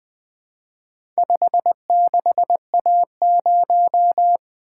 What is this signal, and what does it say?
Morse code, heard as a keyed tone, about 20 wpm, 700 Hz. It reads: 56A0
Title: Four Letter-Number Combinations - 20wpm